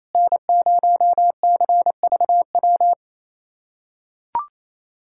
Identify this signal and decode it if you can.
Morse code, heard as a keyed tone, about 28 wpm, 700 Hz. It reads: N0CVW
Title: USA Callsign DE Callsign - 5 Characters - 28wpm